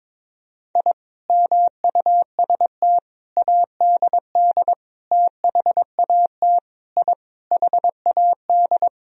Morse code, heard as a keyed tone, about 22 wpm, 700 Hz. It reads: I MUST ADD THAT I HAD
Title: Sentences from Top 1000 Words - Review Part 1 - 22wpm